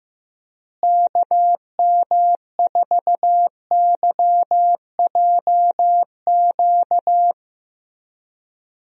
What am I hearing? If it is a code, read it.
Morse code, heard as a keyed tone, about 15 wpm, 700 Hz. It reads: KM4YJQ
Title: USA Call Signs - 6 Characters - Repeated 3x - 15wpm